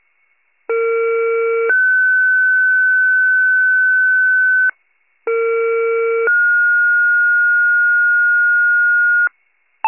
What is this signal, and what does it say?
Two-tone page, heard as a one-second tone, then a three-second tone, about 460 then 1500 Hz, about 460 then 1430 Hz. It is heard from 0.7 to 9.3 s.
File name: pager / 21 FIRE / wcalert